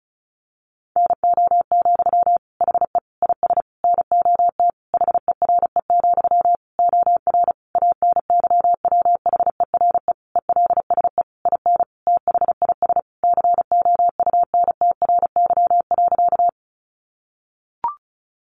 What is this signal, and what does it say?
Morse code, heard as a keyed tone, about 34 wpm, 700 Hz. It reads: NO, HE IS NOT HERE, OR ANYWHERE ELSE IN THIS COUNTRY.
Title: Sentences from Top 1400 Words 35wpm